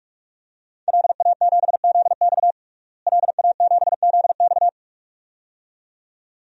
Morse code, heard as a keyed tone, about 46 wpm, 700 Hz. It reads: LA7ZX LA7ZX
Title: DXCC-100 Call Signs - 5 Characters - Repeated 2x - 45wpm